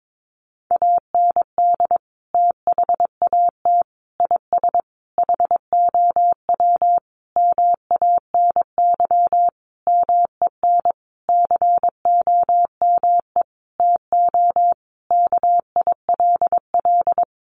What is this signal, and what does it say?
Morse code, heard as a keyed tone, about 22 wpm, 700 Hz. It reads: AND THAT IS HOW MANY MEN COME TO KILL
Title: Sentences from Top 1400 Words 22wpm